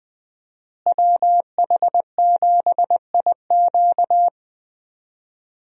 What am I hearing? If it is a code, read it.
Morse code, heard as a keyed tone, about 20 wpm, 700 Hz. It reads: WH7IQ
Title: USA Call Signs - 5 Characters - Repeated 3x - 20wpm